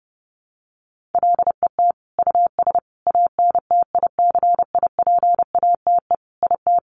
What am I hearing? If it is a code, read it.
Morse code, heard as a keyed tone, about 30 wpm, 700 Hz. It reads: LET US ANTICIPATE IT
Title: Sentences from Top 1500 Words 30wpm